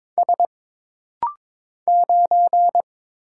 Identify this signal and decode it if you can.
Morse code, heard as a keyed tone, about 22 wpm, 700 Hz. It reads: S 9
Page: This rapid-fire timing is geared to push your ICR (Instant Character Recognition) skills.